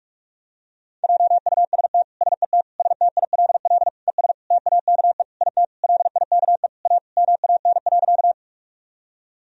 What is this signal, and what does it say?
Morse code, heard as a keyed tone, about 42 wpm, 700 Hz. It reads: JUST SET STILL ES TAKE IT LIKE A MAN.
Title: Sentences from Top 300 Words and Abbreviations 45wpm